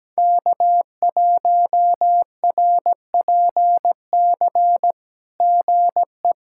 Morse code, heard as a keyed tone, about 17 wpm, 700 Hz. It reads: K1RPC GE
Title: SOTA QSO Practice 17wpm